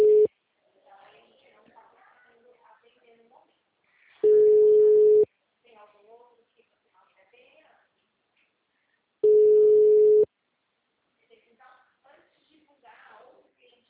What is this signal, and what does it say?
Ringback tone (Europe, 425 Hz), with speech.